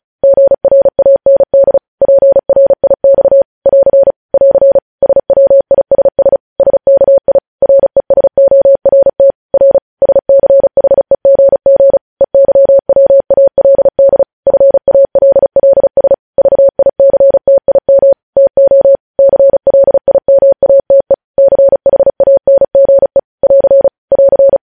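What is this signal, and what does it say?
Morse code, heard as a keyed tone, about 36 wpm, 550 Hz. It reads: GRAND PRIX <AR> <AR> SWISS SKI RESORT R SCHEGG EYWALD FALLS VICTIM TO CLIMATE CHANGE <AR> <AR>
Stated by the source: Morse practice at 35 wpm sending news headlines.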